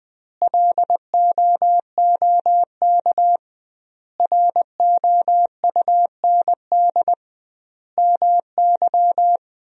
Morse code, heard as a keyed tone, about 20 wpm, 700 Hz. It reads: LOOK ROUND MY
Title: 2x Word-Spacing - Sentences from Top 200 Words - 20wpm